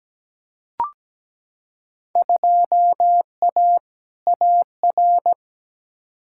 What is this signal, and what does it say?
Morse code, heard as a keyed tone, about 17 wpm, 700 Hz. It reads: 2A AR
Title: Field Day Exchanges - 17wpm